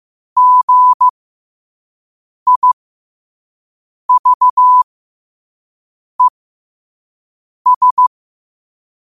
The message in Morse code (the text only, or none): GIVES